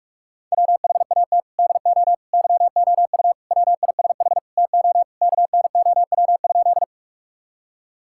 Morse code, heard as a keyed tone, about 42 wpm, 700 Hz. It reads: WHAT DO YOU WISH TO KNOW?
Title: Sentences from Top 1600 Words 45wpm